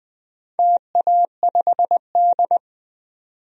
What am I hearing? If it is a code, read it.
Morse code, heard as a keyed tone, about 20 wpm, 700 Hz. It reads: TA5D
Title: DXCC-100 Call Signs - 4 Characters - 20wpm